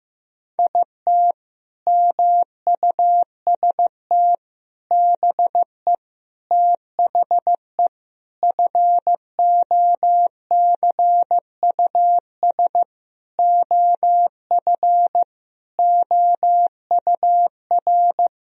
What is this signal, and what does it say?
Morse code, heard as a keyed tone, about 16 wpm, 700 Hz. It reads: IT MUST BE THE FOCUS OF OUR
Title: Sentences from Top 1000 Words - Review Part 4 - 15wpm